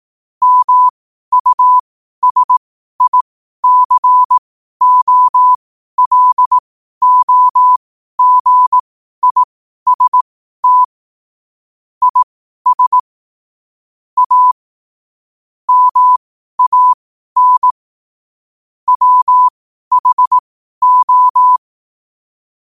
Morse code, heard as a keyed tone, about 18 wpm, spaced as for 12 wpm, 1000 Hz. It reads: MUSICOLOGIST IS A MAN WHO
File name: qotdfiles / 12wpm